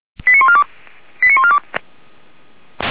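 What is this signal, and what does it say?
Five-tone selective call (ZVEI), heard as quick runs of five tones, tones of about 70 ms. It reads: 89142 89142
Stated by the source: ZVEI five-tone selective call received by SDR for Feuerwehr Orlamünde.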